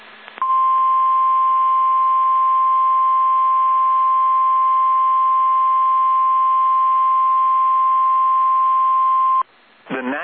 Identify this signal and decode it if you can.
Emergency Alert System (SAME) tones (the 1050 Hz attention tone). It is heard from 0.4 to 9.5 s.